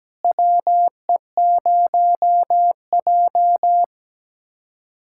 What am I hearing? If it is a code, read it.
Morse code, heard as a keyed tone, about 17 wpm, 700 Hz. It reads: WE0J
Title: USA Callsign DE Callsign - 4 Characters - 17wpm